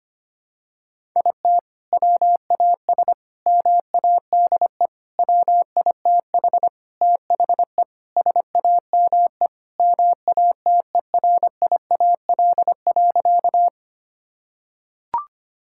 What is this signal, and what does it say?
Morse code, heard as a keyed tone, about 25 wpm, 700 Hz. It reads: IT WAS MADE WITH THE SAME MATERIAL.
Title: Sentences from Top 1000 Words - Review Part 5 - 25wpm